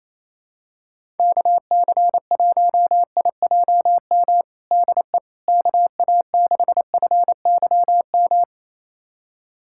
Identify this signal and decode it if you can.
Morse code, heard as a keyed tone, about 28 wpm, 700 Hz. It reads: KC1IJM DE KA6FYM
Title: USA Callsign DE Callsign - 6 Characters - 28wpm